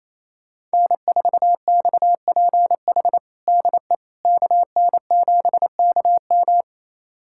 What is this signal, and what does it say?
Morse code, heard as a keyed tone, about 28 wpm, 700 Hz. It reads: N4XPH DE KN7KM
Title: USA Callsign DE Callsign - 5 Characters - 28wpm